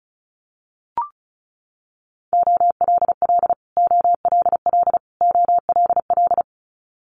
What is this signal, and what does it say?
Morse code, heard as a keyed tone, about 36 wpm, 700 Hz. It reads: OLL OLL OLL
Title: Multi-letter Phonemes Repeated 3x 35wpm